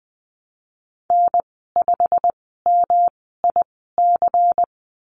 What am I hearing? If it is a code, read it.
Morse code, heard as a keyed tone, about 20 wpm, spaced as for 15 wpm, 700 Hz. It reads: N5MIC
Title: 186 Lesson 30g - Letter b - Call Signs 20wpm